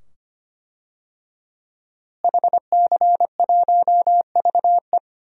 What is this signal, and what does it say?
Morse code, heard as a keyed tone, about 25 wpm, 700 Hz. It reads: HC1VE
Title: DXCC-100 Call Signs - 5 Characters - Repeated 2x - 25wpm